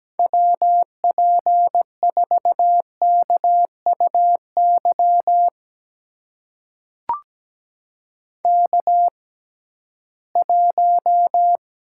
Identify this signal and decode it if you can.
Morse code, heard as a keyed tone, about 17 wpm, spaced as for 13 wpm, 700 Hz. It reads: WP4KUY K1
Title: USA Call Signs - 6x Character Spacing - 17wpm